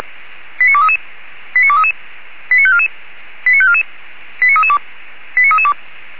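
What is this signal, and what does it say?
Five-tone selective call (ZVEI), heard as quick runs of five tones, tones of about 70 ms. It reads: 89233 89233 89544 89544 89332 89332
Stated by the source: ZVEI five-tone selective call received by SDR, from the feuerwehr-orlamuende fire brigade audio archive.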